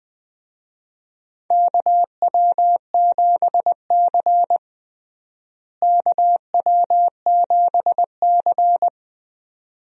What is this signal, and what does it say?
Morse code, heard as a keyed tone, about 20 wpm, 700 Hz. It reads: KW7C KW7C
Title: USA Call Signs - 4 Characters - Repeated 2x - 20wpm